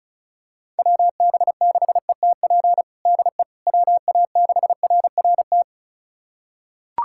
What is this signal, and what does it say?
Morse code, heard as a keyed tone, about 36 wpm, 700 Hz. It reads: WB6ETP DE WA6RRT
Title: USA Callsign DE Callsign - 6 Characters - 35wpm